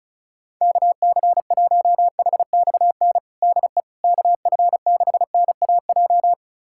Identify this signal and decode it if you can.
Morse code, heard as a keyed tone, about 36 wpm, 700 Hz. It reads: KC1HXN DE KF6NAJ
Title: USA Callsign DE Callsign - 6 Characters - 35wpm